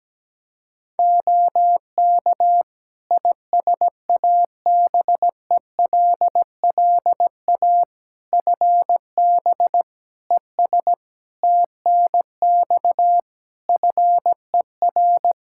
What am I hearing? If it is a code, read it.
Morse code, heard as a keyed tone, about 17 wpm, 700 Hz. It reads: OK ISABELLA FB ES TNX FER
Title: Practice QSO - Exchange 2 - Info - 17wpm